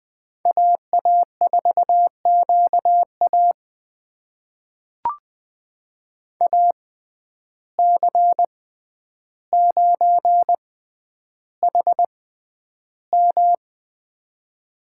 Morse code, heard as a keyed tone, about 20 wpm, spaced as for 10 wpm, 700 Hz. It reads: AA4QA AC9HM
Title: USA Call Signs - 5 Characters - 6x Character Spacing - 20wpm